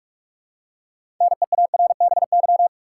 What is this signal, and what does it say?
Morse code, heard as a keyed tone, about 42 wpm, 700 Hz. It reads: NEARBY
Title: Six Letter Words - 45wpm